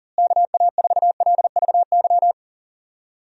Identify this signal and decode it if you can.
Morse code, heard as a keyed tone, about 40 wpm, 700 Hz. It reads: KA4LVY
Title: USA Call Signs - 6 Characters - Repeated 2x - 40wpm